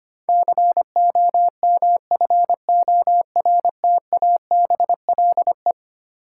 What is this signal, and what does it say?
Morse code, heard as a keyed tone, about 25 wpm, 700 Hz. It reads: COMFORTABLE
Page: morse-code-ninja-practice-25wpm